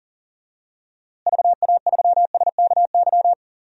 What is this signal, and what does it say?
Morse code, heard as a keyed tone, about 40 wpm, 700 Hz. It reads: VA3SKY
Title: DXCC-100 Call Signs - 6 Characters - Repeated 3x - 40wpm